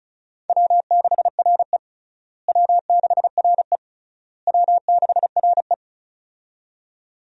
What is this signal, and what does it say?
Morse code, heard as a keyed tone, about 34 wpm, 700 Hz. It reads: W6RE W6RE W6RE
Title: USA Call Signs - 4 Characters - Repeated 3x - 35wpm